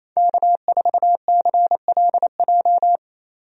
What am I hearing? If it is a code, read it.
Morse code, heard as a keyed tone, about 28 wpm, 700 Hz. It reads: K4CLJ
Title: USA Call Signs - 5 Characters - Repeated 3x - 28wpm